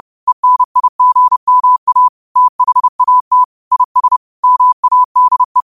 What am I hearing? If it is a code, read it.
Morse code, heard as a keyed tone, about 30 wpm, 1000 Hz. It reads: ENIGMA THAT IS MADE
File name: qotdfiles / 30wpm